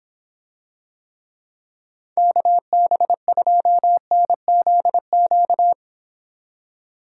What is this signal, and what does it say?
Morse code, heard as a keyed tone, about 26 wpm, 700 Hz. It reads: KB2NZQ